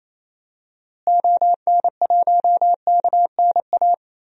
Morse code, heard as a keyed tone, about 28 wpm, 700 Hz. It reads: ON1KNA
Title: DXCC-100 Call Signs - 6 Characters - Repeated 3x - 28wpm